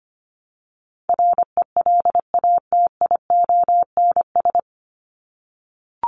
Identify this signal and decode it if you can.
Morse code, heard as a keyed tone, about 25 wpm, 700 Hz. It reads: RELATIONS
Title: ES:S Suffix - Top 100 Words with Base Words 25wpm